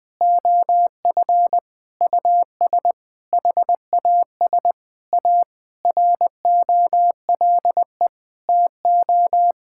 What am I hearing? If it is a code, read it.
Morse code, heard as a keyed tone, about 19 wpm, 700 Hz. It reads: OF US HAS A ROLE TO
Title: Sentences from Top 1000 Words - Review Part 6 - 20wpm